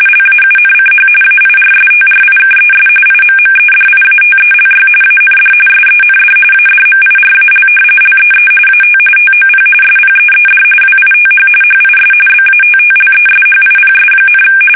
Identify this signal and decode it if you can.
RTTY, about 45.45 baud. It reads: LAUNCH AIRSTRIKES IMMEDIATELY AND SEND ADMIRAL LEE AT TOP SPEED WITH HIS BATTLESHIPS,
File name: fsk850 / battleshipnj